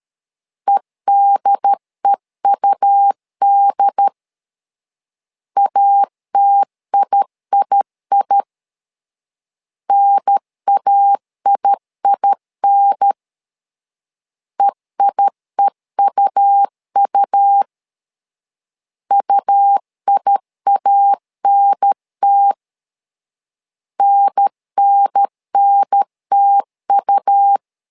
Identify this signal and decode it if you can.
Morse code, heard as a keyed tone, about 13 wpm, 800 Hz. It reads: EDEUD ATIII NAIIN EIEUU UIANT NNNTU
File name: LEZIONI / 2 UHDN / 2 UHDN 12wpm 1456